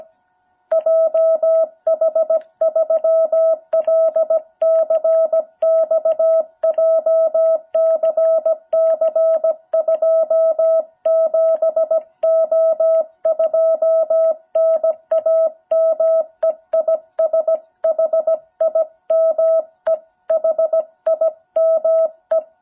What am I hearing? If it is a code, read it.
Morse code, heard as a keyed tone, about 16 wpm, 660 Hz. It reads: JH3LCX JCC27O2 NAME IS HIME HIME
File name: jh8cbh / 2021cw